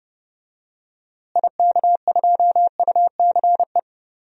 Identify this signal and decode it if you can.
Morse code, heard as a keyed tone, about 32 wpm, 700 Hz. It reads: IK2UCE
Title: DXCC-100 Call Signs - 6 Characters - Repeated 3x - 30wpm